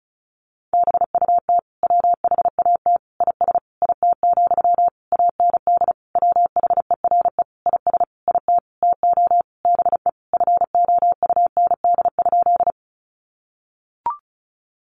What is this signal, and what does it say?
Morse code, heard as a keyed tone, about 34 wpm, 700 Hz. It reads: BUT WHAT IS IT, AND WHERE IS IT TO BE FOUND?
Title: Sentences from Top 300 Words 35wpm